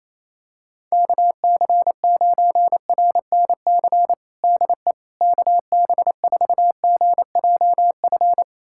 Morse code, heard as a keyed tone, about 28 wpm, 700 Hz. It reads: KC9RNC DE KB4GJF
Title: USA Callsign DE Callsign 28wpm